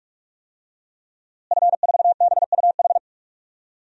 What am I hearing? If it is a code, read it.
Morse code, heard as a keyed tone, about 41 wpm, 700 Hz. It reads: F4BUH
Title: DXCC-100 Call Signs - 5 Characters - Repeated 3x - 45wpm